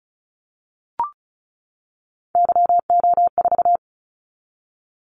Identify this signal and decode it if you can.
Morse code, heard as a keyed tone, about 36 wpm, 700 Hz. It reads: YO4
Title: Callsign Prefixes for Top 25 DXCC - 35wpm